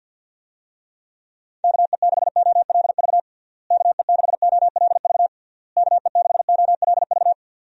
Morse code, heard as a keyed tone, about 52 wpm, 700 Hz. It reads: KE6OLV KE6OLV KE6OLV
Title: USA Call Signs - 6 Characters - Repeated 3x - 50wpm